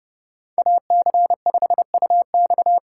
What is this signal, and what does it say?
Morse code, heard as a keyed tone, about 39 wpm, 700 Hz. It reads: AC5UX